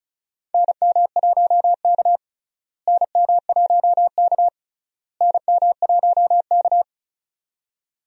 Morse code, heard as a keyed tone, about 36 wpm, 700 Hz. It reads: NM1K NM1K NM1K
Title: USA Call Signs - 4 Characters - Repeated 3x - 35wpm